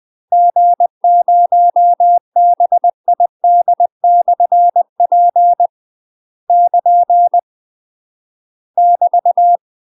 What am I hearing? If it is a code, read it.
Morse code, heard as a keyed tone, about 20 wpm, 700 Hz. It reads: G0BID/P <KN> <BT>
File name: UK MASTER QSOs TRAINING / 20WPM / UK MASTER QSOs 0052